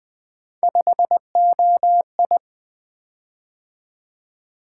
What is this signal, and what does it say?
Morse code, heard as a keyed tone, about 20 wpm, 700 Hz. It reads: 5OI
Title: Three Letter-Number Combinations - 20wpm